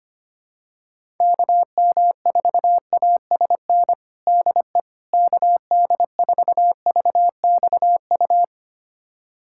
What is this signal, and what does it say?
Morse code, heard as a keyed tone, about 25 wpm, 700 Hz. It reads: KM4ASN DE KD4VXU
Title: USA Callsign DE Callsign - 6 Characters - 25wpm